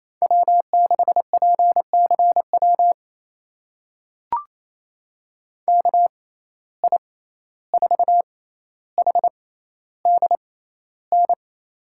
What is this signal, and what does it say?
Morse code, heard as a keyed tone, about 28 wpm, spaced as for 14 wpm, 700 Hz. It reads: W6PCW KI4HDN
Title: USA Call Signs - 6x Character Spacing - 28wpm